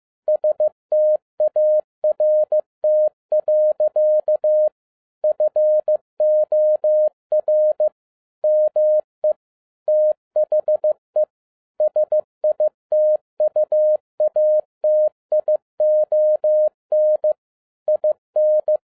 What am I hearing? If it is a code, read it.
Morse code, heard as a keyed tone, about 15 wpm, 610 Hz. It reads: START. FOR ME THE SITUATION IN
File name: 2011-10-mp3 / n02-FIELD DAY CONTEST QRV FROM SAKAE VILLAGE 15wpm